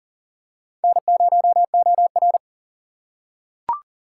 Morse code, heard as a keyed tone, about 41 wpm, 700 Hz. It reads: N0OR
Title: USA Call Signs - 4 Characters - 2x Character Spacing - 40wpm